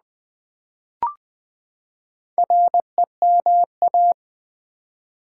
Morse code, heard as a keyed tone, about 20 wpm, 700 Hz. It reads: REMA
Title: Build A Word - 4 Letters At A Time - 10-Letter Words - 20wpm